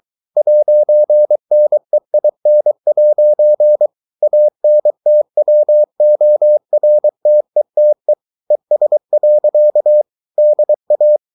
Morse code, heard as a keyed tone, about 23 wpm, 600 Hz. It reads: 'NEIN' ANTWORTETE ES. DA